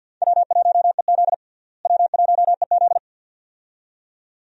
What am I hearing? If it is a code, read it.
Morse code, heard as a keyed tone, about 50 wpm, 700 Hz. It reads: W1EZ W1EZ
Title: USA Call Signs - 4 Characters - Repeated 2x - 50wpm